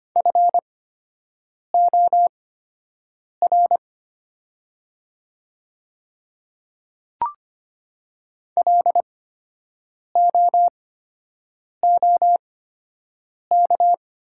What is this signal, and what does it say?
Morse code, heard as a keyed tone, about 25 wpm, spaced as for 7 wpm, 700 Hz. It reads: FOR LOOK
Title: 8x Character Spacing - Top 100 Words - 25wpm